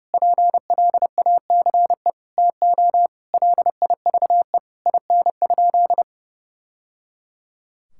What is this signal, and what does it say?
Morse code, heard as a keyed tone, about 30 wpm, 700 Hz. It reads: PLACE TO LIVE IN?
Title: Sentences from Top 1000 Words - Review Part 5 - 30wpm